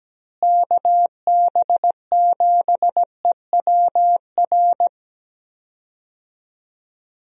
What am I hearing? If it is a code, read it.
Morse code, heard as a keyed tone, about 17 wpm, 700 Hz. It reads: KB7EWR
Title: USA Call Signs - 6 Characters - Repeated 2x - 17wpm